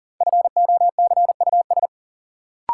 Morse code, heard as a keyed tone, about 40 wpm, 700 Hz. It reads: FOCUS
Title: Sentences from Top 1000 Words - Review Part 4 - New Word Drill - 40wpm